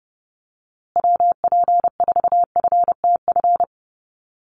Morse code, heard as a keyed tone, about 30 wpm, 700 Hz. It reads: WP4FTF
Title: USA Call Signs Repeated 3x 30wpm